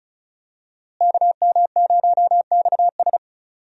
Morse code, heard as a keyed tone, about 36 wpm, 700 Hz. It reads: KM0XS
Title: USA Call Signs - 5 Characters - Repeated 2x - 35wpm